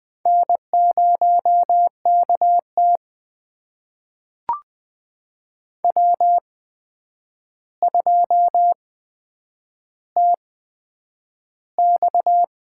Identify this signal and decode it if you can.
Morse code, heard as a keyed tone, about 20 wpm, spaced as for 9 wpm, 700 Hz. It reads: N0KT W2TX
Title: USA Call Signs - 4 Characters - 8x Character Spacing - 20wpm